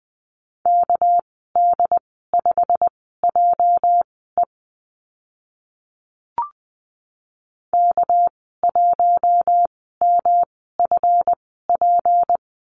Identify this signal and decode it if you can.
Morse code, heard as a keyed tone, about 20 wpm, spaced as for 14 wpm, 700 Hz. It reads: KD5JE K1MFP
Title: 231 Lesson 37g - Letter x - Call Signs 20wpm